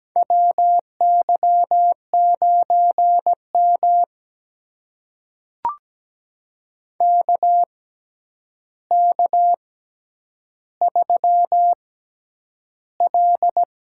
Morse code, heard as a keyed tone, about 17 wpm, spaced as for 9 wpm, 700 Hz. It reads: WY9M KK3L
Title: USA Call Signs - 4 Characters - 6x Character Spacing - 17wpm